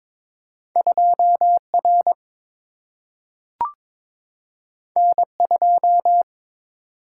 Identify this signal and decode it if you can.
Morse code, heard as a keyed tone, about 22 wpm, 700 Hz. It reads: 2R N2
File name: practice / 22wpm / Two Letter-Number Combinations - 22wpm